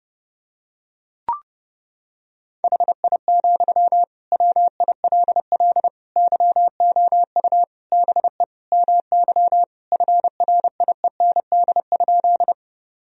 Morse code, heard as a keyed tone, about 30 wpm, 700 Hz. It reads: HI, WILL YOU BE MY FRIEND?
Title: Sentences from Top 1000 Words - Review Part 3 - 30wpm